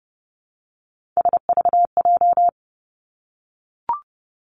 Morse code, heard as a keyed tone, about 32 wpm, 700 Hz. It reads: SVJ
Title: Three Challenging Letter-Number Combinations 30wpm